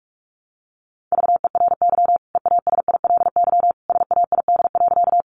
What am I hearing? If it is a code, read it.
Morse code, heard as a keyed tone, about 44 wpm, 700 Hz. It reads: VERY EASILY SAID.
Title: Sentences from Top 900 Words 45wpm